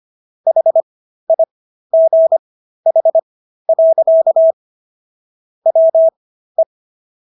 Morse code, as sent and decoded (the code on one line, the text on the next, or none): .... .. --. .... .-.-.- / .-- .
HIGH. WE